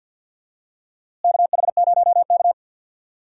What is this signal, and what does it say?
Morse code, heard as a keyed tone, about 50 wpm, 700 Hz. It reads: KH0K
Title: USA Call Signs - 4 Characters - Repeated 2x - 50wpm